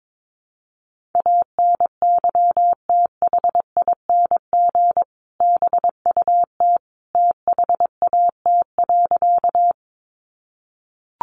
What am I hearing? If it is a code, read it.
Morse code, heard as a keyed tone, about 22 wpm, 700 Hz. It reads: ANYTHING BUT THAT.
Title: Sentences from Top 1000 Words 22wpm